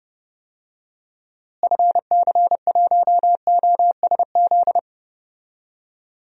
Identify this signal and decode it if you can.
Morse code, heard as a keyed tone, about 30 wpm, 700 Hz. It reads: FC1OSZ
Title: DXCC-100 Call Signs - 6 Characters - Repeated 3x - 30wpm